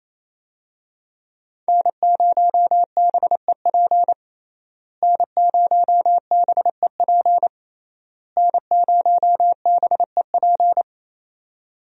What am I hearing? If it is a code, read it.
Morse code, heard as a keyed tone, about 28 wpm, 700 Hz. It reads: N0BEP N0BEP N0BEP
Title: USA Call Signs - 5 Characters - Repeated 3x - 28wpm